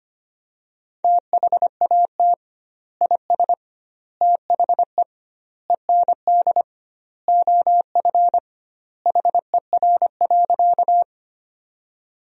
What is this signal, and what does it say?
Morse code, heard as a keyed tone, about 25 wpm, 700 Hz. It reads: THAT IS THE END OF HER.
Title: 2x Word-Spacing - Sentences from Top 200 Words - 25wpm